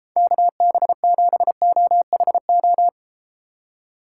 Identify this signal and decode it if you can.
Morse code, heard as a keyed tone, about 32 wpm, 700 Hz. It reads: KB7OHO